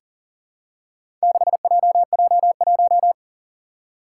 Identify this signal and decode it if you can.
Morse code, heard as a keyed tone, about 40 wpm, 700 Hz. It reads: 6JJ1
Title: Four Challenging Letter-Number Combinations 40wpm